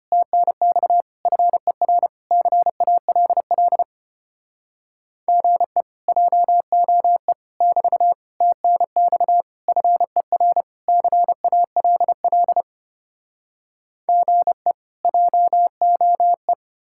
Morse code, heard as a keyed tone, about 33 wpm, 700 Hz. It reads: TNX FER CALL GE JOE <BT> TNX FER CALL GE JOE